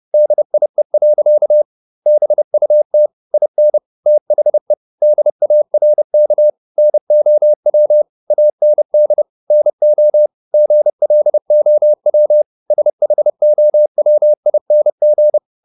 Morse code, heard as a keyed tone, about 30 wpm, 600 Hz. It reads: DIE. BUT IN THE DARK NOW AND NO GLOW SHOWING